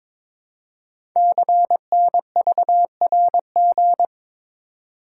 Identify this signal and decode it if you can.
Morse code, heard as a keyed tone, about 22 wpm, 700 Hz. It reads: CNVRG
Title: Code Groups - 5 Letters - 22wpm